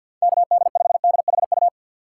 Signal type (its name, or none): Morse code